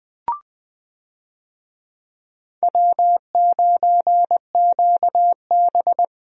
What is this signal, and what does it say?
Morse code, heard as a keyed tone, about 21 wpm, 700 Hz. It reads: W9QB
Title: USA Call Signs - 4 Characters - Repeated 2x - 20wpm